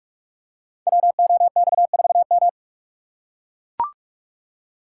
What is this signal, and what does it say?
Morse code, heard as a keyed tone, about 44 wpm, 690 Hz. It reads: WOX4M
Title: Code Groups - 5 Letter:Numbers - 45wpm